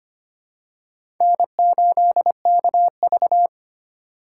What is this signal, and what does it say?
Morse code, heard as a keyed tone, about 25 wpm, 700 Hz. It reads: N8KV
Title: USA Call Signs - 4x Character Spacing - 25wpm